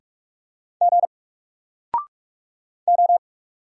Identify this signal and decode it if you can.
Morse code, heard as a keyed tone, about 45 wpm, 700 Hz. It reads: G O
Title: Single Letter-Number 45wpm